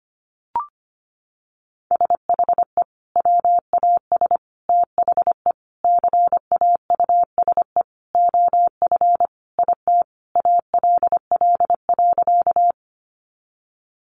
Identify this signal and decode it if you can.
Morse code, heard as a keyed tone, about 25 wpm, 700 Hz. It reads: SHE WAS THE CAUSE OF IT ALL.
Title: Sentences from Top 200 Words 25wpm